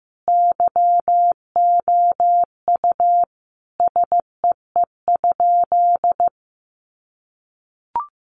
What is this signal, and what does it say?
Morse code, heard as a keyed tone, about 15 wpm, 700 Hz. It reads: YOU SEE?
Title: Sentences from Top 100 Words 15wpm